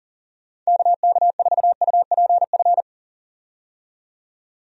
Morse code, heard as a keyed tone, about 37 wpm, 700 Hz. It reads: KK4UPF
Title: USA Call Signs - 6 Characters - 40wpm